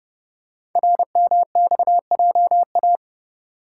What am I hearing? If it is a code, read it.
Morse code, heard as a keyed tone, about 30 wpm, 700 Hz. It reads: RMXJA
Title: Code Groups - 5 Letters - 30wpm